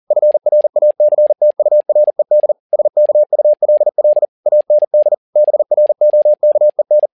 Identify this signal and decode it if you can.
Morse code, heard as a keyed tone, about 40 wpm, 600 Hz. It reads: FRACTURED SKULL AND BROKEN
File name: learning center / tower trouble / tower touble 40wpm